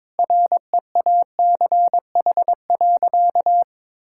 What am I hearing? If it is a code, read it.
Morse code, heard as a keyed tone, about 22 wpm, 700 Hz. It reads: REACH.
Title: Sentences from Top 1000 Words - Review Part 6 - 22wpm